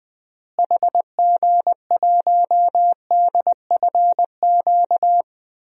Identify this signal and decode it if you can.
Morse code, heard as a keyed tone, about 20 wpm, 700 Hz. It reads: HG1DFQ
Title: DXCC-100 Call Signs - 6 Characters - Repeated 3x - 20wpm